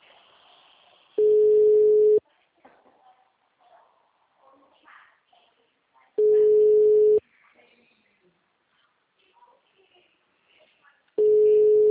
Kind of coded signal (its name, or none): telephone call progress tone